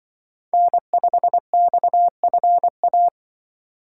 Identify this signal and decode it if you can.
Morse code, heard as a keyed tone, about 24 wpm, 700 Hz. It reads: N5XFA
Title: Speed Racing - USA Call Signs 20wpm